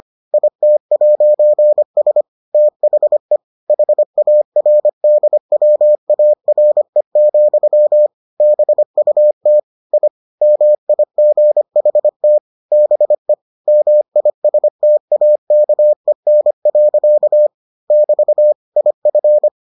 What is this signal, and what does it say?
Morse code, heard as a keyed tone, about 25 wpm, 600 Hz. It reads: IT'S THE HARDWARE, BUT I MIGHT BE MISTAKEN. <BT> IF